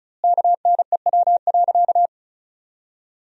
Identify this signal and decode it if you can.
Morse code, heard as a keyed tone, about 36 wpm, 700 Hz. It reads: KNEW.
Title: 2x Word-Spacing - Sentences from Top 400 Words - 35wpm